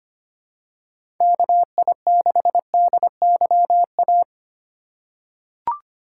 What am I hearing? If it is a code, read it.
Morse code, heard as a keyed tone, about 25 wpm, 700 Hz. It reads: KI6DYA
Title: USA Call Signs - 8x Character Spacing - 25wpm